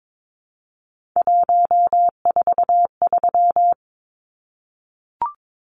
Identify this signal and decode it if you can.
Morse code, heard as a keyed tone, about 22 wpm, 700 Hz. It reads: 143
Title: Numbers - 3-Digit 22wpm